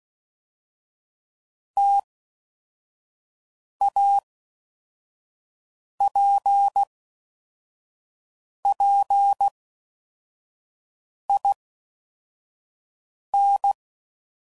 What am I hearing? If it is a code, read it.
Morse code, heard as a keyed tone, about 16 wpm, 800 Hz. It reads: T A P P I N
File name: Combinations / Difficult Combos / T AOEI's